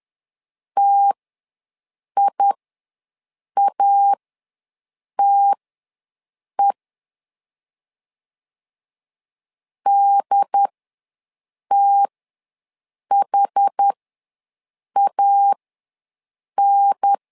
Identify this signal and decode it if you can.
Morse code, heard as a keyed tone, about 10 wpm, spaced as for 6 wpm, 800 Hz. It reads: TIATE DTHAN
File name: 2 UHDN ws 10wpm